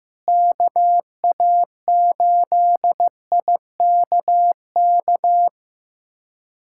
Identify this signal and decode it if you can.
Morse code, heard as a keyed tone, about 15 wpm, 700 Hz. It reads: KA8IKK
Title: USA Callsign DE Callsign - 6 Characters - 15wpm